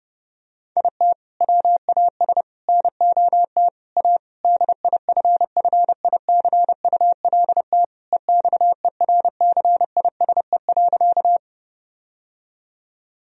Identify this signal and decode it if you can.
Morse code, heard as a keyed tone, about 30 wpm, 700 Hz. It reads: IT WAS NOT A DIFFICULT EXERCISE.
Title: Sentences from Top 1000 Words - Review Part 4 - 30wpm